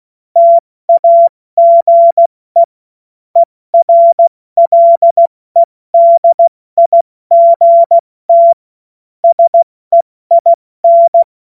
Morse code, heard as a keyed tone, about 16 wpm, 680 Hz. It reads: TAGE ERLEDIGT SEIN
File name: Morsekurs / Witze / Witz 25. Oktober 2022 70 F80